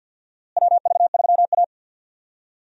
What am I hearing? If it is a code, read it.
Morse code, heard as a keyed tone, about 50 wpm, 700 Hz. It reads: WV3A
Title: USA Call Signs - 4 Characters - Repeated 3x - 50wpm